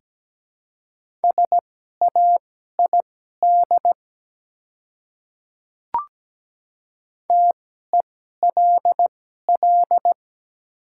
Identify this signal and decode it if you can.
Morse code, heard as a keyed tone, about 17 wpm, spaced as for 12 wpm, 700 Hz. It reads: SAID TELL
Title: 2x Character Spacing - Top 200 Words - 17wpm